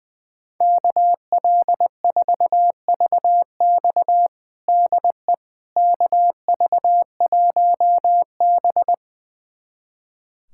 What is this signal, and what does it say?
Morse code, heard as a keyed tone, about 20 wpm, 700 Hz. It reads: KL4VX DE KV1B
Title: USA Callsign DE Callsign 20wpm